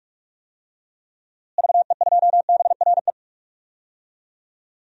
Morse code, heard as a keyed tone, about 46 wpm, 700 Hz. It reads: VE2BRE
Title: DXCC-100 Call Signs - 6 Characters - Repeated 3x - 45wpm